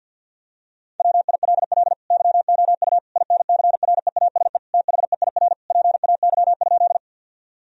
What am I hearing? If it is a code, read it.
Morse code, heard as a keyed tone, about 50 wpm, 700 Hz. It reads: WILL YOU INCREASE THEIR PAY?
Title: Sentences from Top 1000 Words - Review Part 4 - 50wpm